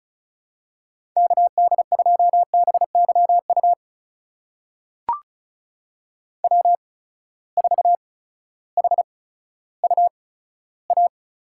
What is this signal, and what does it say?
Morse code, heard as a keyed tone, about 36 wpm, spaced as for 16 wpm, 700 Hz. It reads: KD2BYU W4HUA
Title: USA Call Signs - 8x Character Spacing - 35wpm